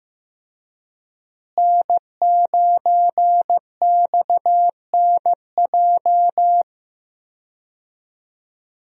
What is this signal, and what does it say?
Morse code, heard as a keyed tone, about 14 wpm, 700 Hz. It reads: N9XNJ
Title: USA Call Signs - 5 Characters - Repeated 3x - 15wpm